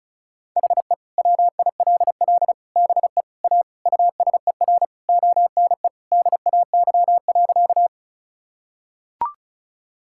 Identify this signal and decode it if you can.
Morse code, heard as a keyed tone, about 36 wpm, 700 Hz. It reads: HE WILL BE A USER ONE DAY.
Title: Sentences from Top 1000 Words - Review Part 5 - 35wpm